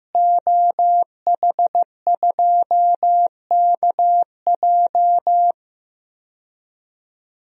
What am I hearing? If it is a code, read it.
Morse code, heard as a keyed tone, about 15 wpm, 700 Hz. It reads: OH2KJ
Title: DXCC-100 Call Signs - 5 Characters - Repeated 2x - 15wpm